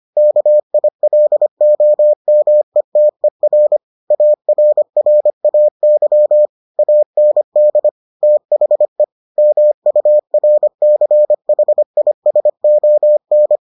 Morse code, heard as a keyed tone, about 25 wpm, 600 Hz. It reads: KILOMETER ARRAY AND THE MURCHISON